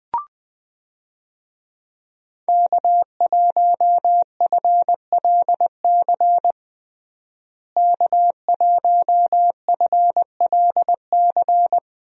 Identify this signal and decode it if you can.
Morse code, heard as a keyed tone, about 20 wpm, 700 Hz. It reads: K1FLC K1FLC
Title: USA Call Signs - 5 Characters - Repeated 2x - 20wpm